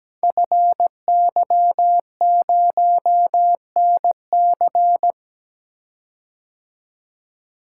Morse code, heard as a keyed tone, about 17 wpm, 700 Hz. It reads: FY0NC
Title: Code Groups - 5 Letter:Numbers - 17wpm